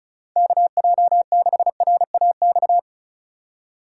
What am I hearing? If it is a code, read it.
Morse code, heard as a keyed tone, about 36 wpm, 700 Hz. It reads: KJ6RAX
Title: USA Call Signs - 6 Characters - 4x Character Spacing - 35wpm